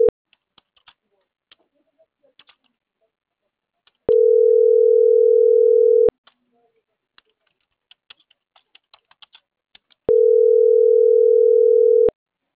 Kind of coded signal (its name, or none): telephone call progress tone